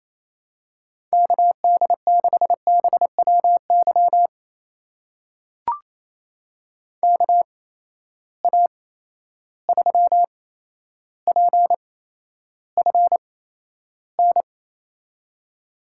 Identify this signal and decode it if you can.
Morse code, heard as a keyed tone, about 28 wpm, spaced as for 12 wpm, 700 Hz. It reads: KD6BWY KA3PFN
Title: USA Call Signs - 6 Characters - 8x Character Spacing - 28wpm